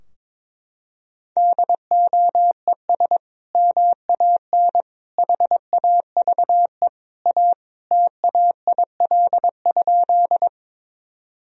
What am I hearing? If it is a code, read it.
Morse code, heard as a keyed tone, about 22 wpm, 700 Hz. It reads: DOES MAN HAVE A TAIL?
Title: Sentences from Top 1000 Words - Review Part 3 - 22wpm